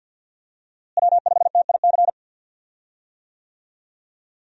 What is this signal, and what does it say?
Morse code, heard as a keyed tone, about 50 wpm, 700 Hz. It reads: W5TIC
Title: USA Call Signs - 5 Characters - Repeated 2x - 50wpm